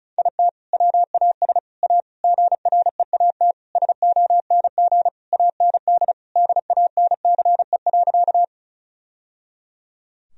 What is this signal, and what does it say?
Morse code, heard as a keyed tone, about 34 wpm, 700 Hz. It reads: IT WAS A GREAT SONG AND DANCE.
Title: Sentences from Top 1000 Words - Review Part 4 - 35wpm